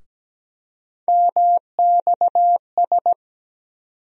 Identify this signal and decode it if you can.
Morse code, heard as a keyed tone, about 17 wpm, 700 Hz. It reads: MXS
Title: USA Call Sign Postfixes - 3 Letters 17wpm